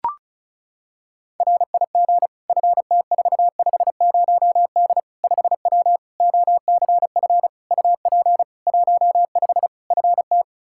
Morse code, heard as a keyed tone, about 36 wpm, 700 Hz. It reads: RIG FT450D 5W OCF UP 15 FT
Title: K1USN SST QSO Practice 35wpm